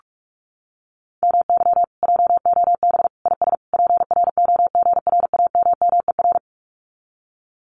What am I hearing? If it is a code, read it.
Morse code, heard as a keyed tone, about 46 wpm, 700 Hz. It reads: MY JOB IS PROGRAMMER
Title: Common QSO Phrases Encore 45wpm